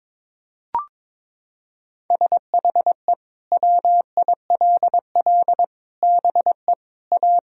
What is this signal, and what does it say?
Morse code, heard as a keyed tone, about 22 wpm, 700 Hz. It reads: SHE WILL BE A
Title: Sentences from Top 1000 Words - Review Part 4 - 22wpm